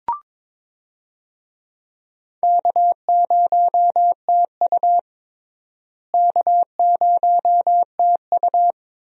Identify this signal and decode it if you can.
Morse code, heard as a keyed tone, about 22 wpm, 700 Hz. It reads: K0TU K0TU
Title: USA Call Signs - 4 Characters - Repeated 2x - 22wpm